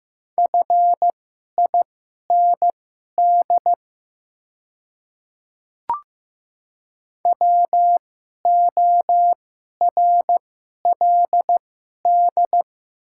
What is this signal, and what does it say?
Morse code, heard as a keyed tone, about 15 wpm, spaced as for 11 wpm, 700 Hz. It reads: FIND WORLD
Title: 2x Character Spacing - Top 300 Words - 15wpm